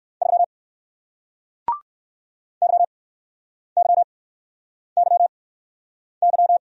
Morse code, heard as a keyed tone, about 61 wpm, spaced as for 9 wpm, 700 Hz. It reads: 3 YYYY